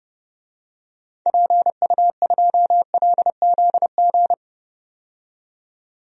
Morse code, heard as a keyed tone, about 30 wpm, 700 Hz. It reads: PU2LZG
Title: DXCC-100 Call Signs - 6 Characters - 30wpm